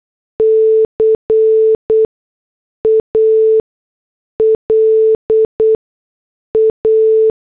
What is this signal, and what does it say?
Morse code, heard as a keyed tone, about 8 wpm, spaced as for 6 wpm, 440 Hz. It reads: CALA